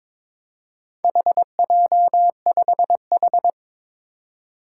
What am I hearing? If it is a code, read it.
Morse code, heard as a keyed tone, about 22 wpm, 700 Hz. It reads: HJ5H
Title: Four Challenging Letter-Number Combinations 22wpm